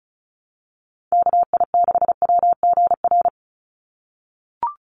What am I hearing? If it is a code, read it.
Morse code, heard as a keyed tone, about 34 wpm, 700 Hz. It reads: KI6WGR
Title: USA Call Signs 35wpm